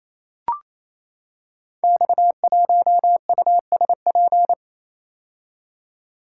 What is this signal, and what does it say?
Morse code, heard as a keyed tone, about 29 wpm, 700 Hz. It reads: X1USP
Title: Code Groups - 5 Letter:Numbers - 28wpm